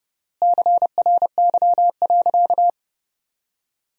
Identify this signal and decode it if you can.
Morse code, heard as a keyed tone, about 30 wpm, 700 Hz. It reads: CRY.
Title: 3x Word-Spacing - Sentences from Top 500 Words - 30wpm